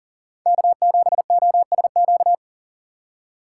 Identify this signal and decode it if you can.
Morse code, heard as a keyed tone, about 40 wpm, 700 Hz. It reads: K7OSQ
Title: USA Call Signs - 5 Characters - 4x Character Spacing - 40wpm